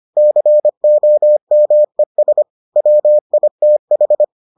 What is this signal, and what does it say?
Morse code, heard as a keyed tone, about 25 wpm, 600 Hz. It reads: COMES WITH